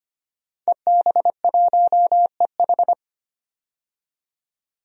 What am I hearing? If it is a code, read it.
Morse code, heard as a keyed tone, about 25 wpm, 700 Hz. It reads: EB1EH
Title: DXCC-100 Call Signs - 5 Characters - 25wpm